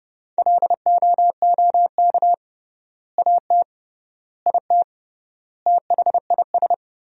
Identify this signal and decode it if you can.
Morse code, heard as a keyed tone, about 32 wpm, 700 Hz. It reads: LOOK AT IT THIS
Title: 3x Word-Spacing - Sentences from Top 100 Words - 30wpm